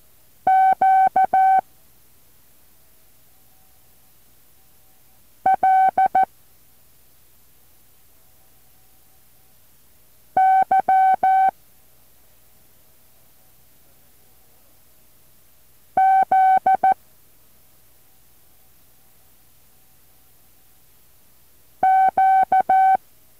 Morse code, heard as a keyed tone, about 14 wpm, 760 Hz. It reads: Q L Y Z Q
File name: intl morse code2